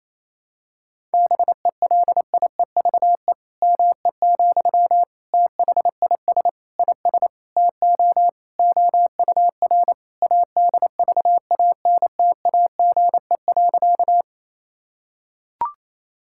Morse code, heard as a keyed tone, about 28 wpm, 700 Hz. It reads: BELIEVE ME, THIS IS TO OUR ADVANTAGE.
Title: Sentences from Top 1000 Words - Review Part 6 - 28wpm